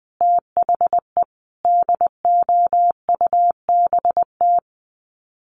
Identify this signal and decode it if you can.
Morse code, heard as a keyed tone, about 20 wpm, 700 Hz. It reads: THE DOUBT
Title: Sets of 5 Words Encore 20wpm